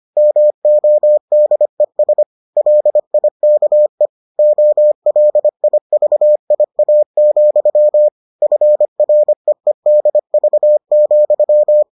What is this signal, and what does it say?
Morse code, heard as a keyed tone, about 25 wpm, 600 Hz. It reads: MODES LIKE OLIVIA, FREEDV,